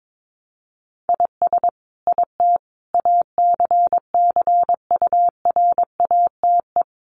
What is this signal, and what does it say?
Morse code, heard as a keyed tone, about 22 wpm, 700 Hz. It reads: IS IT ACCURATE
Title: Sentences from Top 1200 Words 22wpm